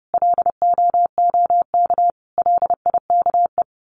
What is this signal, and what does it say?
Morse code, heard as a keyed tone, about 30 wpm, 700 Hz. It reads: LOOK LIKE
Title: Sets of 5 Words Encore2 30wpm